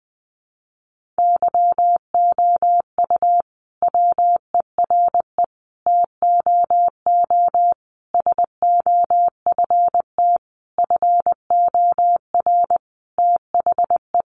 Morse code, heard as a keyed tone, about 20 wpm, 700 Hz. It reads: YOU WERE TOO SOFT FOR THE
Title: Sentences from Top 1500 Words 20wpm